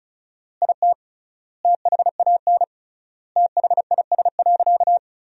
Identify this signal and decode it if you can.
Morse code, heard as a keyed tone, about 36 wpm, 700 Hz. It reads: IT THAN THIS.
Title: 3x Word-Spacing - Sentences from Top 100 Words - 35wpm